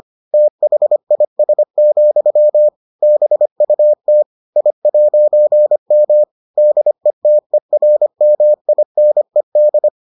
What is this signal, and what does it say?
Morse code, heard as a keyed tone, about 25 wpm, 600 Hz. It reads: THIS, BUT I'M DETERMINED